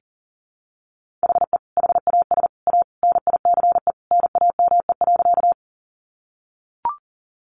Morse code, heard as a keyed tone, about 40 wpm, 700 Hz. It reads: HE HAS A NICE NAME.
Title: Sentences from Top 1100 Words 40wpm